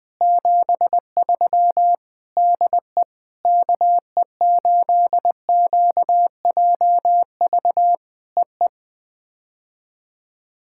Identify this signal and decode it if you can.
Morse code, heard as a keyed tone, about 18 wpm, 700 Hz. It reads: 73 DE KE8QJV EE
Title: POTA QSO Practice - 20wpm